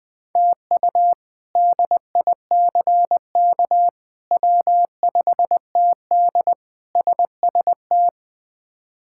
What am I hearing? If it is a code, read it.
Morse code, heard as a keyed tone, about 20 wpm, 700 Hz. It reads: TU DICK W5TD SST
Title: K1USN SST QSO Practice - 20wpm